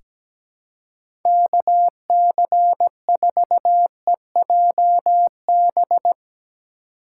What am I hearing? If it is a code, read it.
Morse code, heard as a keyed tone, about 17 wpm, 700 Hz. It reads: KC4EJB
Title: USA Call Signs - 6 Characters - 4x Character Spacing - 17wpm